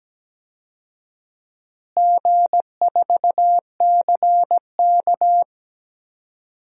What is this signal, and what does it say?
Morse code, heard as a keyed tone, about 17 wpm, 700 Hz. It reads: G4CK
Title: DXCC-100 Call Signs - 4 Characters - Repeated 3x - 17wpm